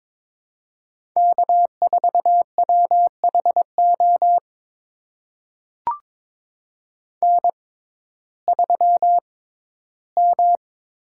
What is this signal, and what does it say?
Morse code, heard as a keyed tone, about 22 wpm, spaced as for 14 wpm, 700 Hz. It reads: K4WHO N3M
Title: USA Call Signs - 5 Characters - 6x Character Spacing - 22wpm